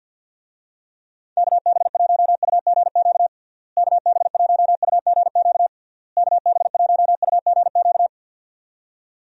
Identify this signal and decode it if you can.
Morse code, heard as a keyed tone, about 51 wpm, 700 Hz. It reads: KB1UGQ KB1UGQ KB1UGQ
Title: USA Call Signs - 6 Characters - Repeated 3x - 50wpm